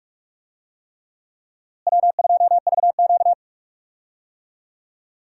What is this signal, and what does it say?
Morse code, heard as a keyed tone, about 43 wpm, 700 Hz. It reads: W2VQ
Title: USA Call Signs - 4 Characters - Repeated 3x - 45wpm